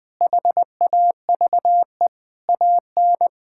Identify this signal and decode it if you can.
Morse code, heard as a keyed tone, about 20 wpm, 700 Hz. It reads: HAVE AN
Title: Sentences with 5-Word Phrases - 20wpm